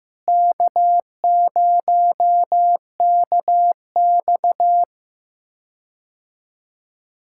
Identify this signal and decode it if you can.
Morse code, heard as a keyed tone, about 15 wpm, 700 Hz. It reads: K0KX
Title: K1USN SST QSO Practice - 15wpm